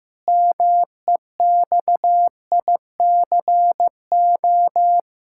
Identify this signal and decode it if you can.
Morse code, heard as a keyed tone, about 15 wpm, 700 Hz. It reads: MEXICO
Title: Pangram Challenge v2 - 15wpm